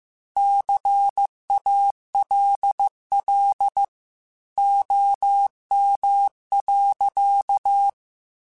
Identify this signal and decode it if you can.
Morse code, heard as a keyed tone, about 15 wpm, 800 Hz. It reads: CALL OM.
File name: QSOs / QSOs-mp3s / kc7sac@13